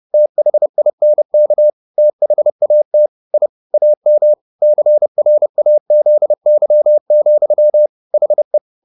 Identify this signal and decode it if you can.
Morse code, heard as a keyed tone, about 30 wpm, 600 Hz. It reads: THINK THAT I AM CRAZY, HE
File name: the old man and the sea / the old man and the sea 1 30wpm